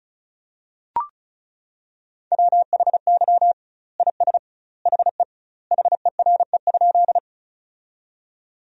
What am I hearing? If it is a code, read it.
Morse code, heard as a keyed tone, about 34 wpm, 700 Hz. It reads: WHY IS HE HERE?
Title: 2x Word-Spacing - Sentences from Top 200 Words - 35wpm